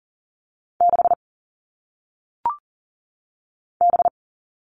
Morse code, heard as a keyed tone, about 39 wpm, 700 Hz. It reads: 6 B
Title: All Cut Numbers - 1-Digit 40wpm